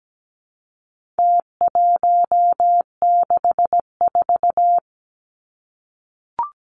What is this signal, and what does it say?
Morse code, heard as a keyed tone, about 17 wpm, 700 Hz. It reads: T164
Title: Numbers - Contest Style - 4-Digit 17wpm